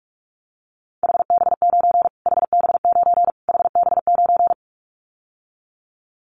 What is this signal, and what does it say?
Morse code, heard as a keyed tone, about 44 wpm, 700 Hz. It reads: HB9 HB9 HB9
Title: Callsign Prefixes for Top 25 DXCC - Repeated 3x - 45wpm